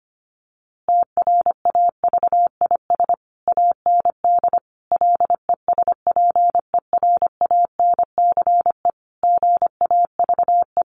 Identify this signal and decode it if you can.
Morse code, heard as a keyed tone, about 25 wpm, 700 Hz. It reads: TRAVIS AND LESPERANCE GAVE
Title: A Sound Of Thunder 25wpm